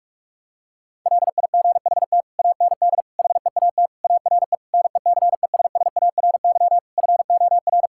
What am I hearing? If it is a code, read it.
Morse code, heard as a keyed tone, about 43 wpm, 700 Hz. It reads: LIGHT AND HEAT ARE NECESSARY FOR
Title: Sentences from Top 1000 Words - Review Part 5 - 45wpm